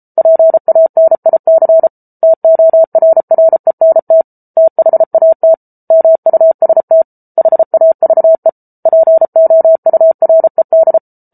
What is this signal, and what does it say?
Morse code, heard as a keyed tone, about 34 wpm, 650 Hz. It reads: PANIC TORRENT THAT MUST HAVE POURED